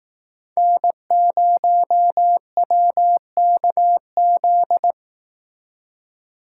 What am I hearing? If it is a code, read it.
Morse code, heard as a keyed tone, about 18 wpm, 700 Hz. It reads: N0WKZ
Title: Speed Racing - USA 1x3 Call Signs 15wpm